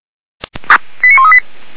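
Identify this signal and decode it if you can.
Five-tone selective call (ZVEI), heard as quick runs of five tones, tones of about 70 ms. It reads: 89138
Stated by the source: Five-tone ZVEI selective call received on SDR at the Feuerwehr Orlamünde fire brigade.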